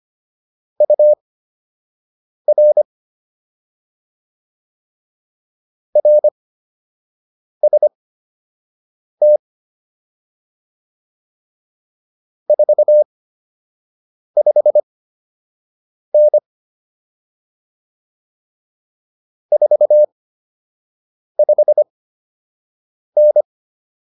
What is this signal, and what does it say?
Morse code, heard as a keyed tone, about 25 wpm, spaced as for 6 wpm, 600 Hz. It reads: UR RST 45N 45N